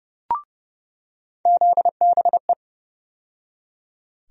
Morse code, heard as a keyed tone, about 29 wpm, 700 Hz. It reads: ZBE
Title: USA Call Sign Postfixes - 3 Letters 30wpm